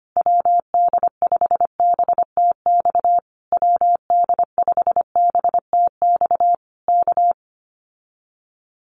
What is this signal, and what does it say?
Morse code, heard as a keyed tone, about 25 wpm, 700 Hz. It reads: WD5BTX WD5BTX K
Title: Common QSO Phrases Encore 25wpm